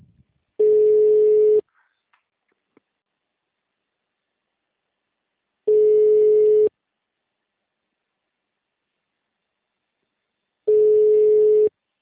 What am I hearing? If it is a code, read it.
Ringback tone (Europe, 425 Hz).